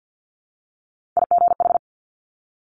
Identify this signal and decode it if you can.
Morse code, heard as a keyed tone, about 50 wpm, 700 Hz. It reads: IGH
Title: Multi-letter Phonemes 50wpm